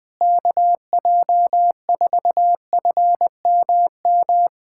Morse code, heard as a keyed tone, about 20 wpm, 700 Hz. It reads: KJ4FMM
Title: USA Callsign DE Callsign - 6 Characters - 20wpm